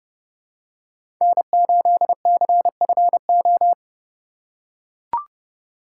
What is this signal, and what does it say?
Morse code, heard as a keyed tone, about 31 wpm, 700 Hz. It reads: N8CFO
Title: USA Call Signs - 5 Characters - 4x Character Spacing - 30wpm